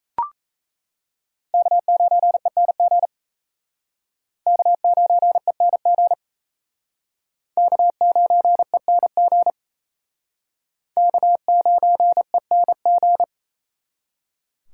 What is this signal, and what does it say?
Morse code, heard as a keyed tone, about 34 wpm, 700 Hz. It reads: K9ENG K9ENG K9ENG K9ENG
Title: Speed Racing - USA 1x3 Call Signs 28wpm